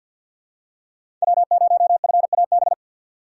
Morse code, heard as a keyed tone, about 52 wpm, 700 Hz. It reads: W0FAB
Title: USA Call Signs - 5 Characters - 3x Character Spacing - 50wpm